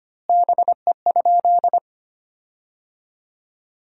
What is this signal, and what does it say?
Morse code, heard as a keyed tone, about 25 wpm, 700 Hz. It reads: BE?
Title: 3x Word-Spacing - Sentences from Top 300 Words - 25wpm